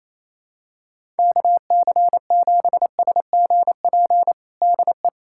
Morse code, heard as a keyed tone, about 28 wpm, 700 Hz. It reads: KC7SGP DE
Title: USA Callsign DE Callsign 28wpm